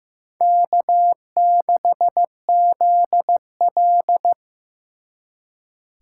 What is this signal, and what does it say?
Morse code, heard as a keyed tone, about 18 wpm, 700 Hz. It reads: K6ZL
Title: Speed Racing - USA 1x2 Call Signs 15wpm